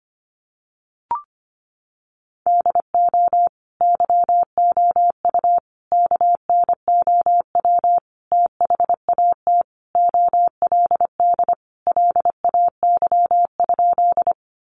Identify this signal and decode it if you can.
Morse code, heard as a keyed tone, about 25 wpm, 700 Hz. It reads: DO YOU KNOW THAT OLD LAY?
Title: Sentences from Top 400 Words 25wpm